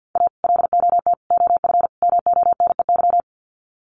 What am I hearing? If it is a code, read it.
Morse code, heard as a keyed tone, about 50 wpm, 700 Hz. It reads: A LOT OF MONEY
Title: Sets of 4 Words 50wpm